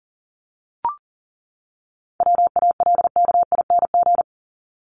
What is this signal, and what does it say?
Morse code, heard as a keyed tone, about 40 wpm, 700 Hz. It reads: WALKING
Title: ING Suffix - Top 100 Words with Base Words 40wpm